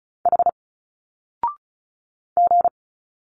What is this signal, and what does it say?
Morse code, heard as a keyed tone, about 34 wpm, 700 Hz. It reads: H G
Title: Single Letters 35wpm